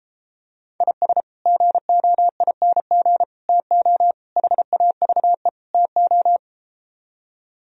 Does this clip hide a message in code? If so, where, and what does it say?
Yes: Morse code from 0.8 to 6.4 s, heard as a keyed tone, about 33 wpm, 700 Hz. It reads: IS GOING TO HAVE TO